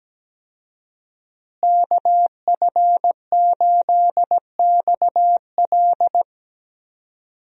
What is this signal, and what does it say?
Morse code, heard as a keyed tone, about 17 wpm, 700 Hz. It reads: KF8XL
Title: USA Call Signs - 5 Characters - Repeated 2x - 17wpm